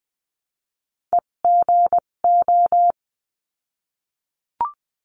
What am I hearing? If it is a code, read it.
Morse code, heard as a keyed tone, about 20 wpm, spaced as for 17 wpm, 700 Hz. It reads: EGO
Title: 257 Lesson 42b - Words 14-20wpm